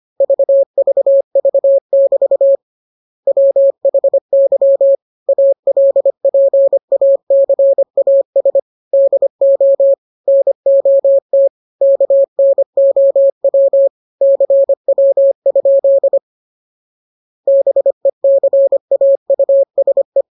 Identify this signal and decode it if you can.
Morse code, heard as a keyed tone, about 25 wpm, 550 Hz. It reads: VVV<BT> WHY ALPACAS DO NOT KNOW CW? BECAUSE